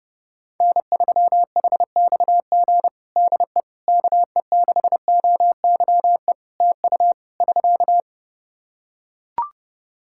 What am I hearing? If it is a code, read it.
Morse code, heard as a keyed tone, about 30 wpm, 700 Hz. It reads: N3HXG DE KE6OYE TU <SK>
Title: Practice QSO - Exchange 4 - Ending - 30wpm